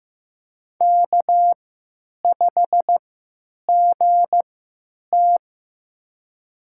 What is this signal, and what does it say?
Morse code, heard as a keyed tone, about 15 wpm, spaced as for 9 wpm, 700 Hz. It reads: K5GT
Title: USA Call Signs - 4 Characters - 3x Character Spacing - 15wpm